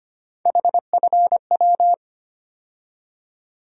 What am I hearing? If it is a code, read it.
Morse code, heard as a keyed tone, about 25 wpm, 700 Hz. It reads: HFW
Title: USA Call Sign Postfixes - 3 Letters 25wpm